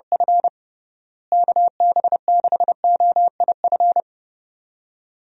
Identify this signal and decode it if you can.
Morse code, heard as a keyed tone, about 32 wpm, 700 Hz. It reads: F KB6OIF
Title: USA Call Signs - 6 Characters - Repeated 2x - 30wpm